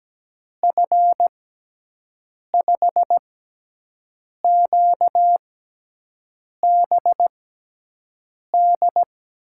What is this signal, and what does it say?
Morse code, heard as a keyed tone, about 17 wpm, spaced as for 6 wpm, 700 Hz. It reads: F5QBD
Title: USA Call Signs - 6x Character Spacing - 17wpm